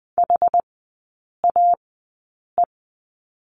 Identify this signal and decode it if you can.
Morse code, heard as a keyed tone, about 20 wpm, spaced as for 8 wpm, 700 Hz. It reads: HAE
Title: 100 Lesson 17e - Letter u - Three Characters 20wpm